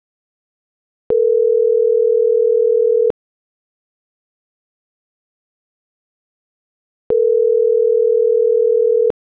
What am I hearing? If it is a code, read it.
Ringback tone (North America), 440 + 480 Hz.